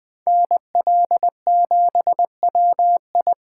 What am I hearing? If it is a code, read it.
Morse code, heard as a keyed tone, about 21 wpm, 700 Hz. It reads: NL7WI
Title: Speed Racing - USA 2x2 Call Signs 17wpm